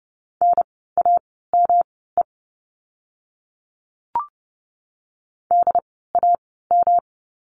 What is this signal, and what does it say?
Morse code, heard as a keyed tone, about 30 wpm, spaced as for 17 wpm, 700 Hz. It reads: NAME DAM
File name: course / 30:10wpm / 108 Lesson 18f - Letter m - Words 30wpm